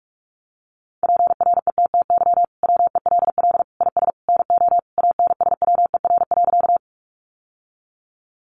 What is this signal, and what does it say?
Morse code, heard as a keyed tone, about 44 wpm, 700 Hz. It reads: PRETTY WELL IS NO ANSWER.
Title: Sentences from Top 1000 Words 45wpm